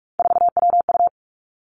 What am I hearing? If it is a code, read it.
Morse code, heard as a keyed tone, about 46 wpm, 700 Hz. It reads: 4WU
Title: Three Challenging Letter-Number Combinations 45wpm